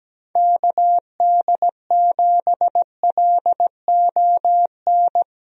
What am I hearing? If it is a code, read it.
Morse code, heard as a keyed tone, about 17 wpm, 700 Hz. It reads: KD7LON
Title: USA Call Signs - 6 Characters - 17wpm